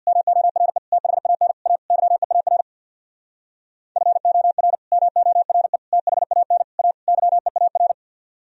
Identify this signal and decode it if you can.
Morse code, heard as a keyed tone, about 65 wpm, 700 Hz. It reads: MORE THAN A YEAR FOR MORE THAN A YEAR